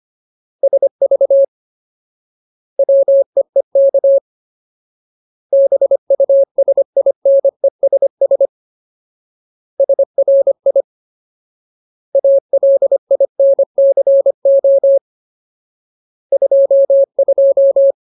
Morse code, heard as a keyed tone, about 25 wpm, 550 Hz. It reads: SV WEEK BUSINESS SRI ALINCO 22
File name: Ultimate Word List / 25WPM Ultimate